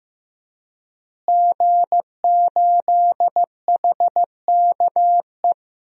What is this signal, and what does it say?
Morse code, heard as a keyed tone, about 15 wpm, 700 Hz. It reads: G8HKE
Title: DXCC-100 Call Signs - 5 Characters - Repeated 3x - 15wpm